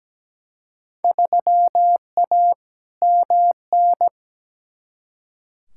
Morse code, heard as a keyed tone, about 17 wpm, 700 Hz. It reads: 3A MN
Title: Field Day Exchanges - 17wpm